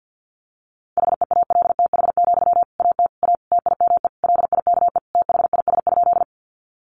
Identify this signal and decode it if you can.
Morse code, heard as a keyed tone, about 50 wpm, 700 Hz. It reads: HEALTH, AT A TIME LIKE THIS?
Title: Sentences from Top 600 Words 50wpm